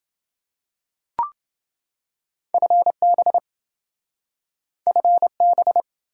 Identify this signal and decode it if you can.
Morse code, heard as a keyed tone, about 27 wpm, 700 Hz. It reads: FB FB